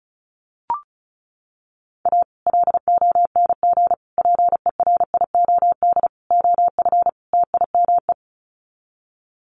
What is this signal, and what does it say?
Morse code, heard as a keyed tone, about 34 wpm, 700 Hz. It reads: A LONG PERIOD OF TIME
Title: Sets of 5 Words 35wpm